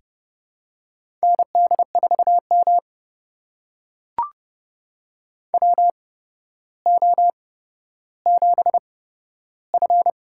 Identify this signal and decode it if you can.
Morse code, heard as a keyed tone, about 29 wpm, spaced as for 13 wpm, 700 Hz. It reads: ND4M WO7F
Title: USA Call Signs - 4 Characters - 8x Character Spacing - 30wpm